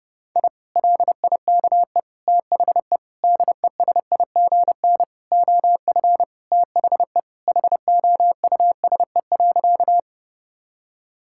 Morse code, heard as a keyed tone, about 30 wpm, 700 Hz. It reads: I LIKE THE DESIGN OF THE HOUSE.
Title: Sentences from Top 1000 Words - Review Part 5 - 30wpm